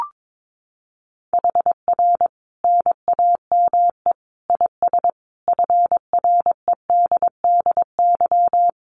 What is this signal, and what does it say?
Morse code, heard as a keyed tone, about 22 wpm, 700 Hz. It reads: HR NAME IS FREDDY
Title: Common QSO Phrases 22wpm